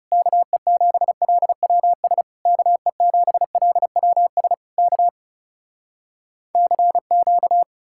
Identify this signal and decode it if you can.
Morse code, heard as a keyed tone, about 40 wpm, 700 Hz. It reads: KE7LWS KE7LWS K CQ